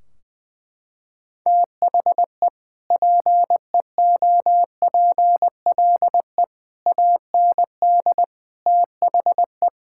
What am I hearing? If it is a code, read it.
Morse code, heard as a keyed tone, about 20 wpm, 700 Hz. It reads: THE PEOPLE AND THE
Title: Sentences from Top 1600 Words 20wpm